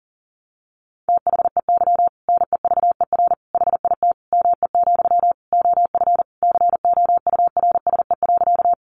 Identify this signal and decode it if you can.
Morse code, heard as a keyed tone, about 40 wpm, 700 Hz. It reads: THEY NEVER HIT ME, OF COURSE.
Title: Sentences from Top 1200 Words 40wpm